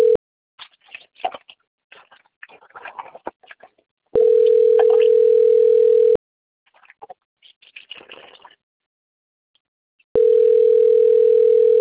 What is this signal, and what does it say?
Ringback tone (North America), 440 + 480 Hz.
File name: rg-1600-unknown- - - .94535